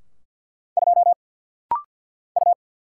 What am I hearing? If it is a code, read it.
Morse code, heard as a keyed tone, about 50 wpm, 700 Hz. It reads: 2 U
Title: Single Letter-Number - Rapid-Fire 50wpm